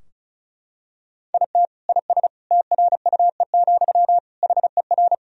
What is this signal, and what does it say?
Morse code, heard as a keyed tone, about 36 wpm, 700 Hz. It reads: IT IS TRUE, HER
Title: Sentences from Top 1000 Words - Review Part 2 - 35wpm